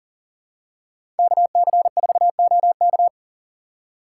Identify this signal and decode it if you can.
Morse code, heard as a keyed tone, about 37 wpm, 700 Hz. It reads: KC4OK
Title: USA Call Signs - 5 Characters - 6x Character Spacing - 40wpm